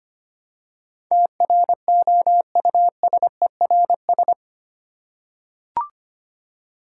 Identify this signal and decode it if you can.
Morse code, heard as a keyed tone, about 25 wpm, 700 Hz. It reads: TROUSERS
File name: 8 Letter Words - First 3 Letters Given - 25wpm